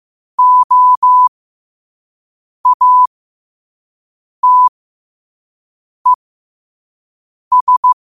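Morse code, heard as a keyed tone, about 15 wpm, spaced as for 5 wpm, 1000 Hz. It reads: OATES
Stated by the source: Morse sent at 5 wpm.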